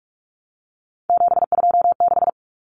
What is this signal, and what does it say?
Morse code, heard as a keyed tone, about 44 wpm, 700 Hz. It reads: 726
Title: Common Cut Numbers - 3-Digit 45wpm